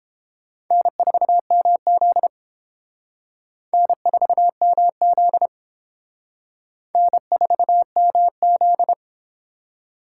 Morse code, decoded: N4MZ N4MZ N4MZ

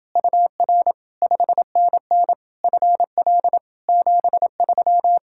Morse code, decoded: UR 5NN FL 73

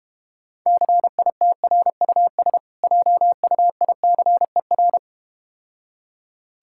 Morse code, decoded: CITRUS JUICER

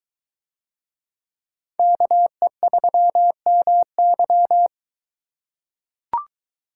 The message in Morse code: KE3MY

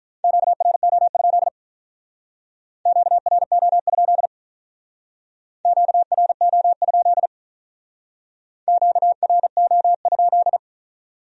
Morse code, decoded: QRO? QRO? QRO? QRO?

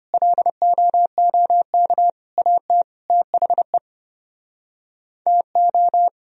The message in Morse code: LOOK AT THE TO

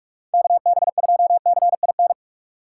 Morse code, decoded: KB2CIN

